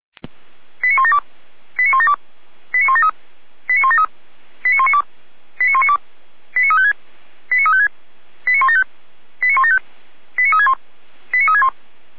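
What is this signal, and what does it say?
Five-tone selective call (ZVEI), heard as quick runs of five tones, tones of about 70 ms. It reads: 89172 89172 89173 89173 89192 89192 89356 89356 89175 89175 89361 89361
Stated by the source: Received with an SDR; ZVEI five-tone call 89013.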